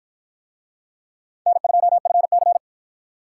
Morse code, heard as a keyed tone, about 53 wpm, 700 Hz. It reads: N2FC